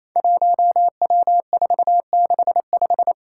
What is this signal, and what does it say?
Morse code, heard as a keyed tone, about 28 wpm, 700 Hz. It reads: 1W465
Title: Five Challenging Letter-Number Combinations 28wpm